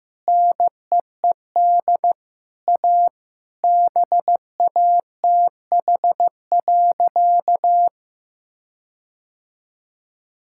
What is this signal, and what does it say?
Morse code, heard as a keyed tone, about 15 wpm, 700 Hz. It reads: NEED A BATH.
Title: Sentences from Top 1600 Words 15wpm